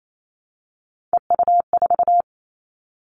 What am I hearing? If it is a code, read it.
Morse code, heard as a keyed tone, about 28 wpm, 700 Hz. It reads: EU4
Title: Callsign Prefixes for Top 100 DXCC - 28wpm